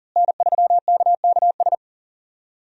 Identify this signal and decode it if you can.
Morse code, heard as a keyed tone, about 40 wpm, 690 Hz. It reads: N3KKS
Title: USA Call Signs - 5 Characters - Repeated 3x - 40wpm